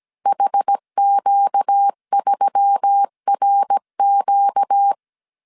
Morse code, heard as a keyed tone, about 17 wpm, 800 Hz. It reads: HQ3RQ
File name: CW / DET / 1 6 1500C S A 16wpm 2635